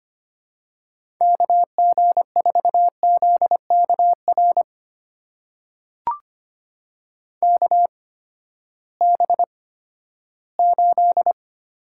Morse code, heard as a keyed tone, about 25 wpm, spaced as for 14 wpm, 700 Hz. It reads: KG4ZKR KB8
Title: USA Call Signs - 6 Characters - 8x Character Spacing - 25wpm